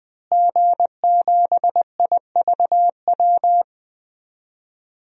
Morse code, heard as a keyed tone, about 20 wpm, 700 Hz. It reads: G7IVW
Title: DXCC-100 Call Signs - 5 Characters - Repeated 3x - 20wpm